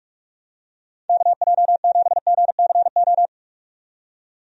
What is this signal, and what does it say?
Morse code, heard as a keyed tone, about 44 wpm, 700 Hz. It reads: KJ7GCO
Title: USA Call Signs - 6 Characters - 45wpm